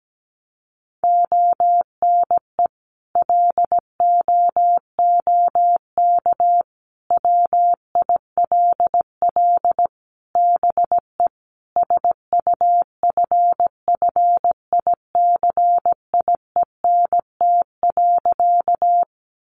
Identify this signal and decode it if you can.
Morse code, heard as a keyed tone, about 17 wpm, 700 Hz. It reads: ONE LOOK WILL BE SUFFICIENT.
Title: Sentences from Top 1400 Words 17wpm